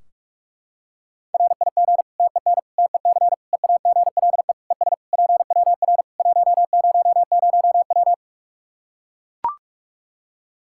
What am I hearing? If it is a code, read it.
Morse code, heard as a keyed tone, about 44 wpm, 700 Hz. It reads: RIG TEN TEC EAGLE ES PWR 100W
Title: Practice QSO - Exchange 2 - Info - 45wpm